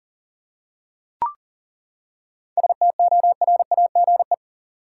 Morse code, heard as a keyed tone, about 40 wpm, 700 Hz. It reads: STORAGE
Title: Seven Letter Words - 40wpm